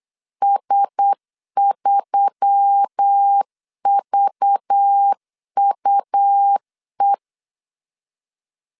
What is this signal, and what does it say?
Morse code, heard as a keyed tone, about 8 wpm, 800 Hz. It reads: S3VUE
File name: LEZIONI / 6 PKX0 / 6 P 8wpm 3025